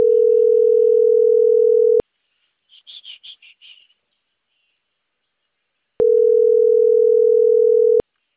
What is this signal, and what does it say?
Ringback tone (North America), 440 + 480 Hz.